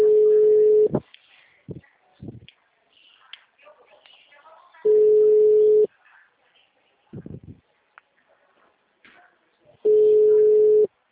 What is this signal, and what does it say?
Ringback tone (Europe, 425 Hz).
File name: out- -2005- - - .62024